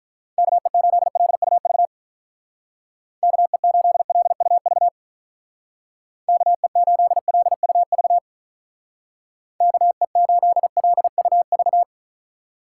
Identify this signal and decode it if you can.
Morse code, heard as a keyed tone, about 44 wpm, 700 Hz. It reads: KE8LUV KE8LUV KE8LUV KE8LUV